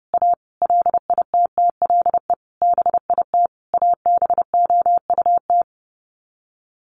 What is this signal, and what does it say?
Morse code, heard as a keyed tone, about 30 wpm, 700 Hz. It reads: A LITTLE BIT ABOUT
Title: Sets of 5 Words Encore 30wpm